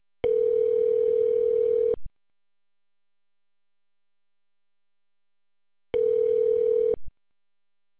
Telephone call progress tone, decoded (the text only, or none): ringback tone